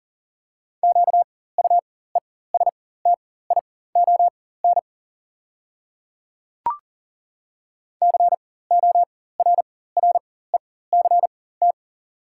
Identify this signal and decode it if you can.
Morse code, heard as a keyed tone, about 37 wpm, spaced as for 18 wpm, 700 Hz. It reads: QUESTION CORRECT
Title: 4x Character Spacing - Top 500 Words - 40wpm